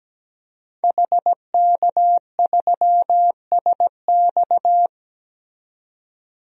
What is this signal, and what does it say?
Morse code, heard as a keyed tone, about 17 wpm, 700 Hz. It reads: HK3SX
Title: DXCC-100 Call Signs - 5 Characters - 17wpm